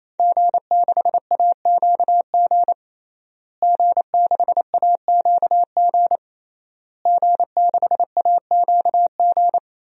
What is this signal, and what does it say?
Morse code, heard as a keyed tone, about 28 wpm, 700 Hz. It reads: G6AQG G6AQG G6AQG
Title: DXCC-100 Call Signs - 5 Characters - Repeated 3x - 28wpm